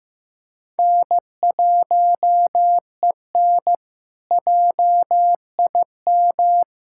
Morse code, heard as a keyed tone, about 16 wpm, 700 Hz. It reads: N1EN JIM
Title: K1USN SST QSO Practice - 15wpm